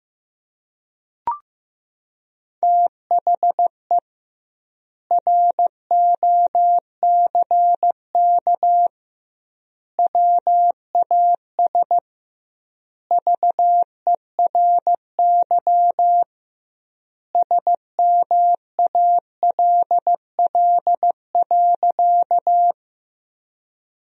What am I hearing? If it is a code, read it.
Morse code, heard as a keyed tone, about 15 wpm, 700 Hz. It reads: THE ROCK WAS VERY SMALL.
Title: 2x Word-Spacing - Sentences from Top 400 Words - 15wpm